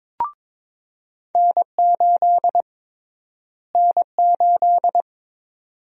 Morse code, heard as a keyed tone, about 22 wpm, 700 Hz. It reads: N8 N8
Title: USA Call Sign Prefixes - Repeated 2x - 22wpm